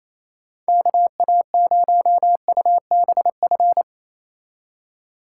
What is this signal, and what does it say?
Morse code, heard as a keyed tone, about 30 wpm, 700 Hz. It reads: KA0UBF
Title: Speed Racing - USA 2x3 Call Signs 28wpm